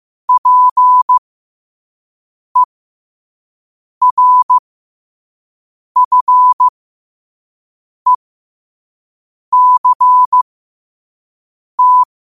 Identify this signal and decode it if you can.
Morse code, heard as a keyed tone, about 15 wpm, spaced as for 5 wpm, 1000 Hz. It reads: PERFECT
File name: qotdfiles / 5wpm